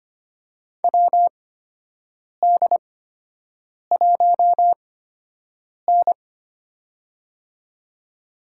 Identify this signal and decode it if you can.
Morse code, heard as a keyed tone, about 25 wpm, spaced as for 7 wpm, 700 Hz. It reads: WD1N
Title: USA Call Signs - 4 Characters - 8x Character Spacing - 25wpm